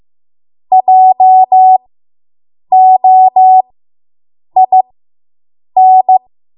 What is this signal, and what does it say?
Morse code, heard as a keyed tone, about 15 wpm, spaced as for 7 wpm, 750 Hz. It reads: JOIN